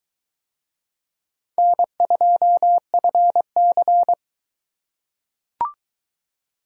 Morse code, heard as a keyed tone, about 22 wpm, 700 Hz. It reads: N2FC